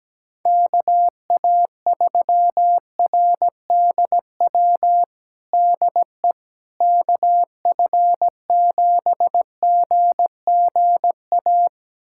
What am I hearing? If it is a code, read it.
Morse code, heard as a keyed tone, about 17 wpm, 700 Hz. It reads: KA3RDW DE KF7GGA
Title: USA Callsign DE Callsign 17wpm